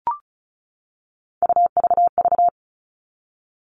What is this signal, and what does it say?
Morse code, heard as a keyed tone, about 36 wpm, 700 Hz. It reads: UVV
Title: Three Challenging Letter-Number Combinations 35wpm